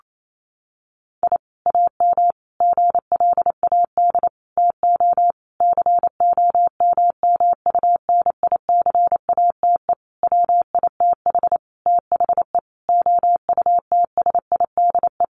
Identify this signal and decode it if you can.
Morse code, heard as a keyed tone, about 28 wpm, 700 Hz. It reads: I AM GLAD TO COMMUNICATE WITH THE OUTSIDE
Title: Sentences from Top 1300 Words 28wpm